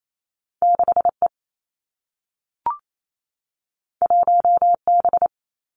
Morse code, heard as a keyed tone, about 28 wpm, 700 Hz. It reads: 6 E 1 B
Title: Two Challenging Letter-Number Combinations 28wpm